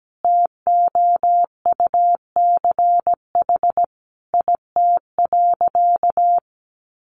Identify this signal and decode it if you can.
Morse code, heard as a keyed tone, about 17 wpm, 700 Hz. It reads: TOUCH IT.
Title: Sentences from Top 1200 Words 17wpm